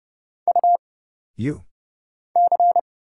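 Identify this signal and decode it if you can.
Morse code, heard as a keyed tone, about 31 wpm, 700 Hz. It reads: U C
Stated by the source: Morse Code Ninja single-character practice (letters, numbers, punctuation, prosigns) at 30 wpm, by Kurt Zoglmann (AD0WE).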